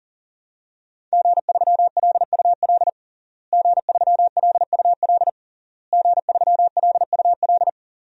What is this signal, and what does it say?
Morse code, heard as a keyed tone, about 37 wpm, 700 Hz. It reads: G3LUL G3LUL G3LUL
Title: DXCC-100 Call Signs - 5 Characters - Repeated 3x - 40wpm